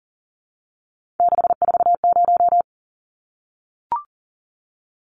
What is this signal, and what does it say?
Morse code, heard as a keyed tone, about 40 wpm, 700 Hz. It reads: 640
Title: Numbers - 3-Digit 40wpm